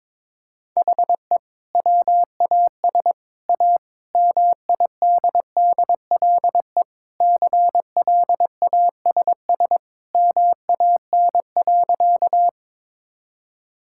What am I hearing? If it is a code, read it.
Morse code, heard as a keyed tone, about 22 wpm, 700 Hz. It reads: HE WAS A MIDDLE CLASS MAN.
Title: Sentences from Top 1000 Words - Review Part 6 - 22wpm